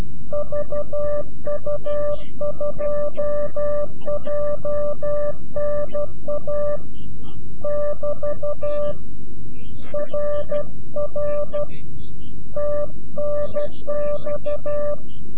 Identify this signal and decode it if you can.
Morse code, heard as a keyed tone, about 12 wpm, 590 Hz. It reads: VU2JNA <BT> RR TNX
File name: CW-QSO / CQ Call / CQ-QSO